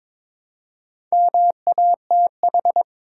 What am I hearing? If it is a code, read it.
Morse code, heard as a keyed tone, about 22 wpm, 700 Hz. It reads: MATH
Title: Four Letter Words - 22wpm